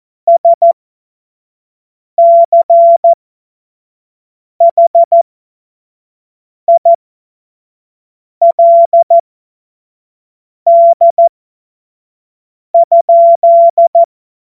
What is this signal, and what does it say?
Morse code, heard as a keyed tone, about 14 wpm, spaced as for 5 wpm, 680 Hz. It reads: SCHILD?
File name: Morsekurs / Witze / Witz 06. April 2021, 25 F70